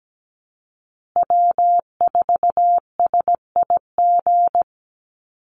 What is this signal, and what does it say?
Morse code, heard as a keyed tone, about 17 wpm, 700 Hz. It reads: W4SIG
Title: Call Signs Repeated 3x - 17wpm